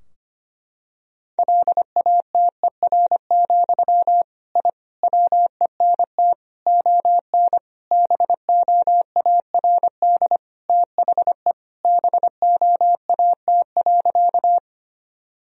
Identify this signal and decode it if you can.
Morse code, heard as a keyed tone, about 25 wpm, 700 Hz. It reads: LATER, I WENT ON BOARD THE BOAT.
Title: Sentences from Top 1000 Words - Review Part 4 - 25wpm